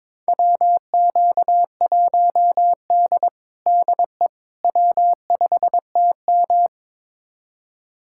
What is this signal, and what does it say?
Morse code, heard as a keyed tone, about 22 wpm, 700 Hz. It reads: WQ1D DE W5TM
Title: USA Callsign DE Callsign - 4 Characters - 22wpm